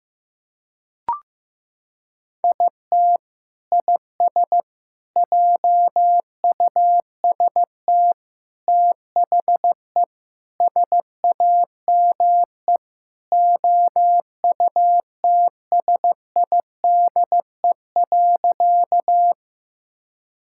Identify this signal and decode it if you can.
Morse code, heard as a keyed tone, about 16 wpm, 700 Hz. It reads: IT IS JUST THE SAME OUTSIDE.
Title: Sentences from Top 1000 Words - Review Part 6 - 15wpm